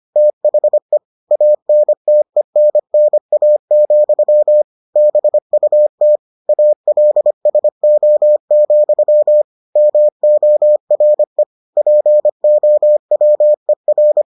Morse code, heard as a keyed tone, about 25 wpm, 600 Hz. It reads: THE ANTENNA, BUT ALSO, MORE POWER